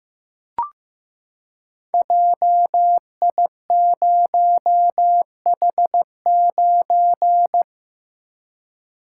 Morse code, heard as a keyed tone, about 15 wpm, 700 Hz. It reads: JI0H9
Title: Code Groups - 5 Letter:Numbers - 15wpm